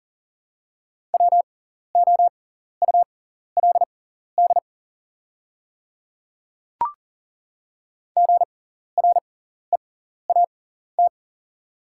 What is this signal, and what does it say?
Morse code, heard as a keyed tone, about 40 wpm, spaced as for 14 wpm, 690 Hz. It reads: WOULD GREAT
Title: 6x Character Spacing - Top 200 Words - 40wpm